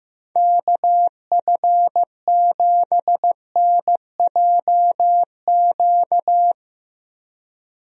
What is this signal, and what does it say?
Morse code, heard as a keyed tone, about 15 wpm, 700 Hz. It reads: KF7NJQ
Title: USA Call Signs - 6 Characters - Repeated 3x - 15wpm